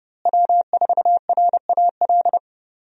Morse code, heard as a keyed tone, about 30 wpm, 700 Hz. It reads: W4RAL
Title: USA Call Signs - 5 Characters - Repeated 2x - 30wpm